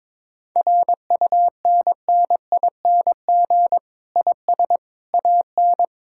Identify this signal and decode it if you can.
Morse code, heard as a keyed tone, about 22 wpm, 700 Hz. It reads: RUNNING IS AN
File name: Difficult Sentences with Hints - 22wpm